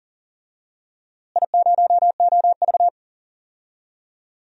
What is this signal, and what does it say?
Morse code, heard as a keyed tone, about 40 wpm, 700 Hz. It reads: I0OV
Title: DXCC-100 Call Signs - 4 Characters - Repeated 2x - 40wpm